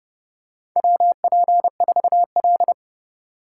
Morse code, heard as a keyed tone, about 30 wpm, 700 Hz. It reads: WP4L
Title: USA Call Signs - 4 Characters - 3x Character Spacing - 30wpm